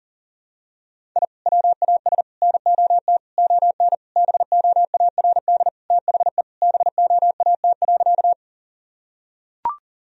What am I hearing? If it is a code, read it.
Morse code, heard as a keyed tone, about 40 wpm, 700 Hz. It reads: I WAS NOT ON BOARD THE BOAT.
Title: Sentences from Top 1000 Words - Review Part 4 - 40wpm